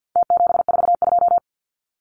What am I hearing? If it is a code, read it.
Morse code, heard as a keyed tone, about 48 wpm, 700 Hz. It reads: T742
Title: Numbers - Contest Style - 4-Digit 50wpm